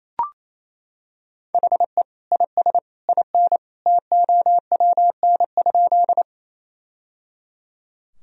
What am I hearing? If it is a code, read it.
Morse code, heard as a keyed tone, about 28 wpm, 700 Hz. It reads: HE IS IN TOWN?
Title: Sentences from Top 1000 Words - Review Part 2 - 28wpm